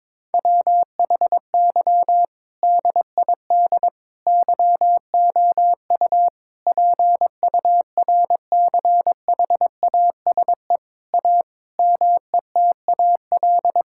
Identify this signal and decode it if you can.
Morse code, heard as a keyed tone, about 22 wpm, 700 Hz. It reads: WHY DID YOU PURCHASE A METAL
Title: Sentences from Top 1000 Words - Review Part 6 - 22wpm